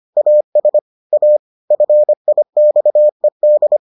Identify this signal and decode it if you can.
Morse code, heard as a keyed tone, about 25 wpm, 600 Hz. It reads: AS A FIXED